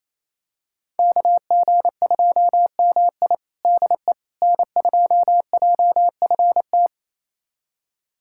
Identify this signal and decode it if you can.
Morse code, heard as a keyed tone, about 28 wpm, 700 Hz. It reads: KG2MI DE N2JFT
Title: USA Callsign DE Callsign 28wpm